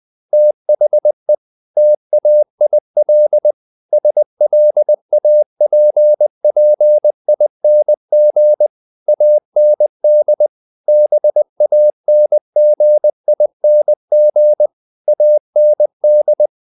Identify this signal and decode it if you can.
Morse code, heard as a keyed tone, about 20 wpm, 600 Hz. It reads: THE TAIL SLAPPING AND BANGING AND
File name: the old man and the sea / the old man and the sea part 1 20wpm